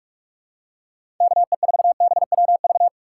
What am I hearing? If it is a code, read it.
Morse code, heard as a keyed tone, about 42 wpm, 700 Hz. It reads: KE4BWV
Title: USA Call Signs - 6 Characters - 2x Character Spacing - 45wpm